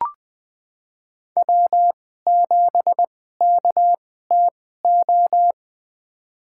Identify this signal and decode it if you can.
Morse code, heard as a keyed tone, about 20 wpm, spaced as for 15 wpm, 700 Hz. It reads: W7KTO
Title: USA Call Signs - 5 Characters - 2x Character Spacing - 20wpm